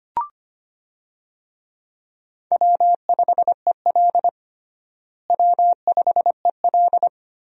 Morse code, heard as a keyed tone, about 25 wpm, 700 Hz. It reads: W5EL W5EL
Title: USA Call Signs - 4 Characters - Repeated 3x - 25wpm